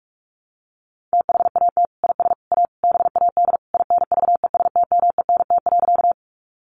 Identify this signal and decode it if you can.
Morse code, heard as a keyed tone, about 44 wpm, 700 Hz. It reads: THAT IS A BAD INVESTMENT.
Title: Sentences from Top 700 Words 45wpm